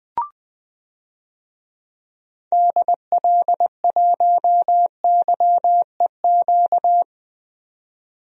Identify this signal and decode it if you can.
Morse code, heard as a keyed tone, about 20 wpm, 700 Hz. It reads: DL1YEQ
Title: DXCC-100 Call Signs - 6 Characters - 20wpm